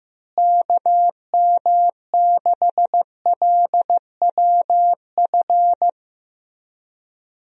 Morse code, heard as a keyed tone, about 15 wpm, 700 Hz. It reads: KM6LWF
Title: USA Call Signs - 6 Characters - Repeated 2x - 15wpm